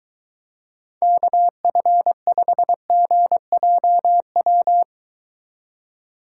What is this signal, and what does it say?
Morse code, heard as a keyed tone, about 22 wpm, 700 Hz. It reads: KF5GJW